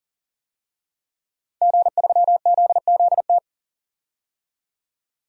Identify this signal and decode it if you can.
Morse code, heard as a keyed tone, about 37 wpm, 700 Hz. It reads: G3ZZT
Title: DXCC-100 Call Signs - 5 Characters - 40wpm